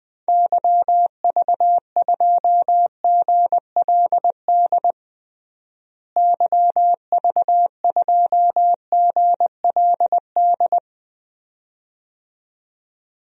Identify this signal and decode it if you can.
Morse code, heard as a keyed tone, about 20 wpm, 700 Hz. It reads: YV2GLD YV2GLD
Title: DXCC-100 Call Signs - 6 Characters - Repeated 2x - 20wpm